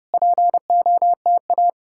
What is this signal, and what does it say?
Morse code, heard as a keyed tone, about 33 wpm, 700 Hz. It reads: POTA